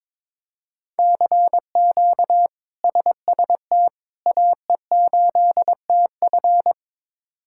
Morse code, heard as a keyed tone, about 22 wpm, 700 Hz. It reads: CQ SST AE8TF
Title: K1USN SST QSO Practice - 22wpm